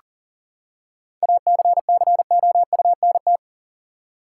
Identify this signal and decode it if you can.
Morse code, heard as a keyed tone, about 38 wpm, 700 Hz. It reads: ACCOUNT
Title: Seven Letter Words - 40wpm